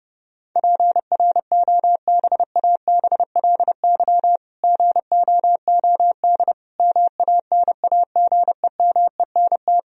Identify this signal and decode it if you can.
Morse code, heard as a keyed tone, about 30 wpm, 700 Hz. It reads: PROBABLY GOOD MANAGEMENT
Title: Sentences from Top 1000 Words - Review Part 4 - 30wpm